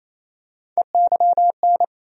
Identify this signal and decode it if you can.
Morse code, heard as a keyed tone, about 28 wpm, 700 Hz. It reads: EYN
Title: USA Call Sign Postfixes - 3 Letters 28wpm